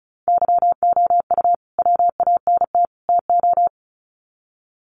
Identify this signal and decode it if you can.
Morse code, heard as a keyed tone, about 34 wpm, 700 Hz. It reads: YOU WANT TO
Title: Sets of 3 Words 35wpm